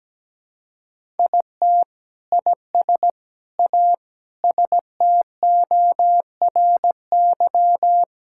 Morse code, heard as a keyed tone, about 17 wpm, 700 Hz. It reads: IT IS A STORY
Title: Sentences from Top 1000 Words - Review Part 5 - 17wpm